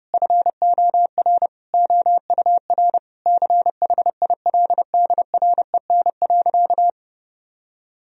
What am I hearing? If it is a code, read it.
Morse code, heard as a keyed tone, about 30 wpm, 700 Hz. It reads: FOR OUR CHILDREN.
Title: Sentences from Top 1000 Words - Review Part 4 - 30wpm